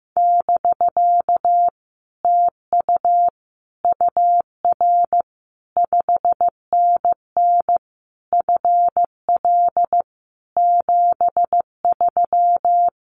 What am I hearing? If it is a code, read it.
Morse code, heard as a keyed tone, about 15 wpm, 700 Hz. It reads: <BK> TU UR 5NN FL 73
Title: Common QSO Phrases Encore 15wpm